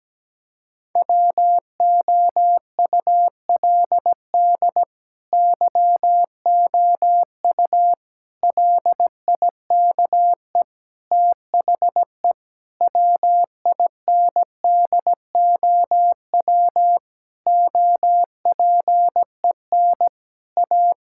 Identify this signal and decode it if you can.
Morse code, heard as a keyed tone, about 17 wpm, 700 Hz. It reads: WOULD YOU LIKE THE WINDOW OPEN A
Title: Sentences from Top 1600 Words 17wpm